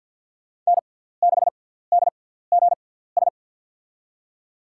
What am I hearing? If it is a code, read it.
Morse code, heard as a keyed tone, about 47 wpm, spaced as for 17 wpm, 700 Hz. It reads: N6DGS
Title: USA Call Signs - 5 Characters - 6x Character Spacing - 50wpm